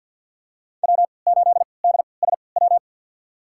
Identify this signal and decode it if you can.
Morse code, heard as a keyed tone, about 52 wpm, spaced as for 29 wpm, 700 Hz. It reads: W8DSW
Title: USA Call Signs - 5 Characters - 3x Character Spacing - 50wpm